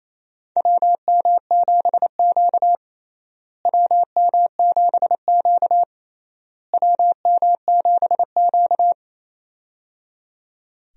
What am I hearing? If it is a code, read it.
Morse code, heard as a keyed tone, about 28 wpm, 700 Hz. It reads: WM7Q WM7Q WM7Q
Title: USA Call Signs - 4 Characters - Repeated 3x - 28wpm